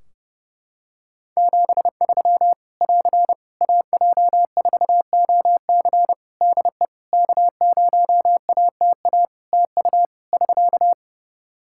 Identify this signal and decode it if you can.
Morse code, heard as a keyed tone, about 31 wpm, 700 Hz. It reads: 73 <AR> AJ4OC DE K0ATA TU <SK>
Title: Practice QSO - Exchange 4 - Ending - 30wpm